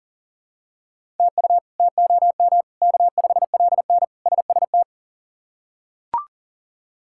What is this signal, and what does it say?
Morse code, heard as a keyed tone, about 40 wpm, 700 Hz. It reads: TU TOM K5LN SST
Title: K1USN SST QSO Practice - 40wpm